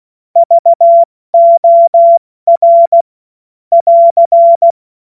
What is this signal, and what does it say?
Morse code, heard as a keyed tone, about 16 wpm, 680 Hz. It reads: VOR <AR>
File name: Morsekurs / Witze / Witz 16. Dezember 2025 70 F80